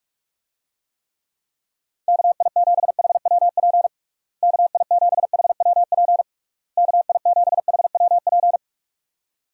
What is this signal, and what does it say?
Morse code, heard as a keyed tone, about 42 wpm, 700 Hz. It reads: KI7HWP KI7HWP KI7HWP
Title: USA Call Signs - 6 Characters - Repeated 3x - 45wpm